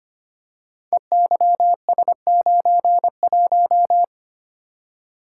Morse code, heard as a keyed tone, about 25 wpm, 700 Hz. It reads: EYS91
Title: Code Groups - 5 Letter:Numbers - 25wpm